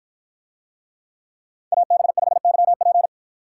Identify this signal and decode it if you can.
Morse code, heard as a keyed tone, about 51 wpm, 700 Hz. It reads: AB5YP